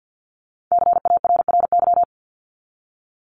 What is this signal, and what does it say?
Morse code, heard as a keyed tone, about 50 wpm, 700 Hz. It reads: CARRY
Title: ING Suffix - Top 100 Words with Base Words 50wpm